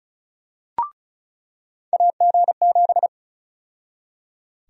Morse code, heard as a keyed tone, about 34 wpm, 700 Hz. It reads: AG7
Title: USA Call Sign Prefixes - 35wpm